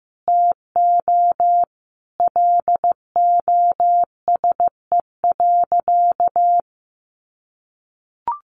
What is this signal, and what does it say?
Morse code, heard as a keyed tone, about 15 wpm, 700 Hz. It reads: TO LOSE.
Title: Sentences from Top 1100 Words 15wpm